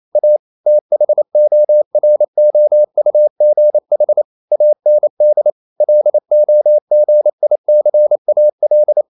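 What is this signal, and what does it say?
Morse code, heard as a keyed tone, about 28 wpm, 600 Hz. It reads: A THOROUGH AND LOGICAL
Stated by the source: Morse code practice at 28 wpm, text of Ken Follett's novel Never in English, from CW med Gnister.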